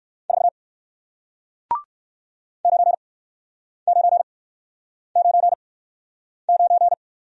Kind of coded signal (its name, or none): Morse code